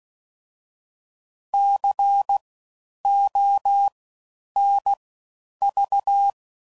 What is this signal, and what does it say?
Morse code, heard as a keyed tone, about 16 wpm, spaced as for 9 wpm, 800 Hz. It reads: CONV
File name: Suffixes / Special INGs / Fast Ings